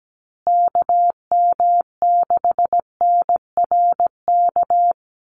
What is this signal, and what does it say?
Morse code, heard as a keyed tone, about 17 wpm, 700 Hz. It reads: KM6NRK
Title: USA Call Signs Repeated 2x 17wpm